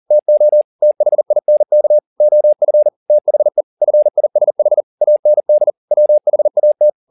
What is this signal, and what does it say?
Morse code, heard as a keyed tone, about 40 wpm, 600 Hz. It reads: TO THINK OF THE FISH AND WHAT
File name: the old man and the sea 2 40wpm